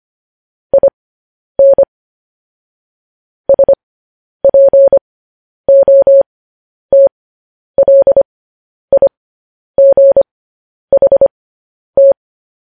Morse code, heard as a keyed tone, about 25 wpm, spaced as for 10 wpm, 550 Hz. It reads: IN SPOTLIGHT
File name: podcast / News headlines-1x-25x10